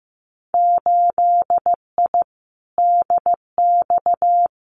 Morse code, heard as a keyed tone, about 15 wpm, 700 Hz. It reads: 8I DX
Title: Winter Field Day Exchanges Encore 15wpm